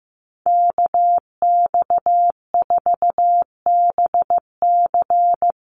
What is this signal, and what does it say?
Morse code, heard as a keyed tone, about 15 wpm, 700 Hz. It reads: KX4BC
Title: USA Call Signs Repeated 2x 15wpm